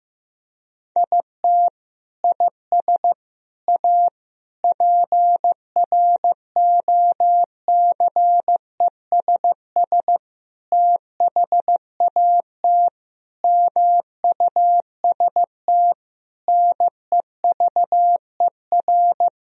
Morse code, heard as a keyed tone, about 15 wpm, 700 Hz. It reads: IT IS A PROCESS THAT MUST NEVER
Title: Sentences from Top 1000 Words - Review Part 3 - 15wpm